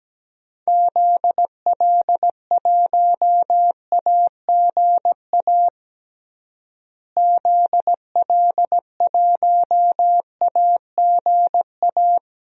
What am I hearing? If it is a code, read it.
Morse code, heard as a keyed tone, about 17 wpm, 700 Hz. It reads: ZL1AGA ZL1AGA
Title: DXCC-100 Call Signs - 6 Characters - Repeated 3x - 17wpm